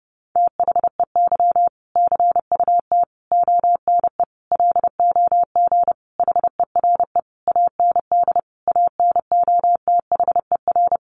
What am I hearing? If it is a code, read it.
Morse code, heard as a keyed tone, about 30 wpm, 700 Hz. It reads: THEY CUT ONE LOG HERE AND ANOTHER
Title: Sentences from Top 1400 Words 30wpm